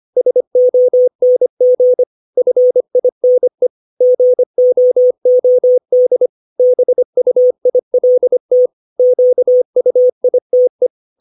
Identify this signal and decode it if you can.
Morse code, heard as a keyed tone, about 25 wpm, 500 Hz. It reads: SONG FINE GOOD BUILT QUITE
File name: 632 words / 632 Words 25WPM